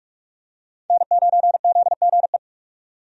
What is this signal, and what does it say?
Morse code, heard as a keyed tone, about 46 wpm, 700 Hz. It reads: N9ZGE
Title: USA Call Signs - 5 Characters - 45wpm